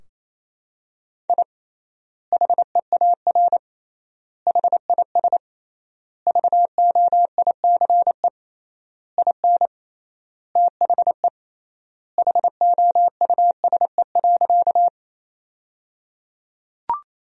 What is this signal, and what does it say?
Morse code, heard as a keyed tone, about 27 wpm, 700 Hz. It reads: I HEAR HIS VOICE IN THE HOUSE.
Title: 3x Word-Spacing - Sentences from Top 500 Words - 28wpm